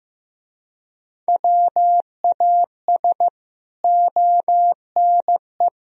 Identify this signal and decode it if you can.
Morse code, heard as a keyed tone, about 15 wpm, 700 Hz. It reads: WAS ONE
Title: Speed-Racing - Sets of 4 Words 15wpm